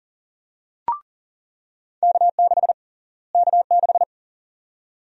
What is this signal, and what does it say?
Morse code, heard as a keyed tone, about 40 wpm, 700 Hz. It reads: K6 K6
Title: USA Call Sign Prefixes - Repeated 2x - 40wpm